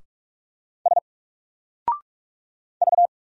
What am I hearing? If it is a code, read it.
Morse code, heard as a keyed tone, about 44 wpm, 700 Hz. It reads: S V
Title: Single Letter-Number 45wpm